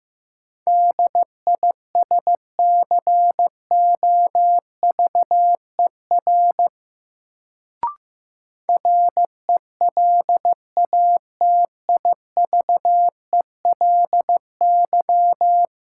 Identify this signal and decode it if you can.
Morse code, heard as a keyed tone, about 15 wpm, 700 Hz. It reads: DISCOVER RELATIVELY
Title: Sentences from Top 1100 Words - New Word Drill - 15wpm